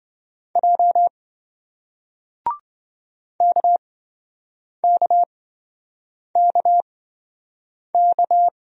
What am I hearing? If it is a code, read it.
Morse code, heard as a keyed tone, about 28 wpm, spaced as for 7 wpm, 700 Hz. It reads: J KKKK